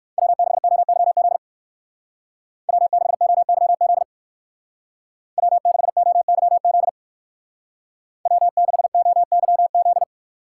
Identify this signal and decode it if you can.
Morse code, heard as a keyed tone, about 56 wpm, 700 Hz. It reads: W6OYZ W6OYZ W6OYZ W6OYZ
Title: Speed Racing - USA 1x3 Call Signs 45wpm